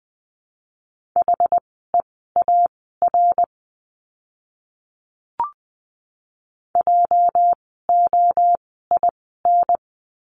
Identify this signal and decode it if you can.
Morse code, heard as a keyed tone, about 20 wpm, spaced as for 14 wpm, 700 Hz. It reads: HEAR JOIN
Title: 206 Lesson 33f - Letter j - Words 20wpm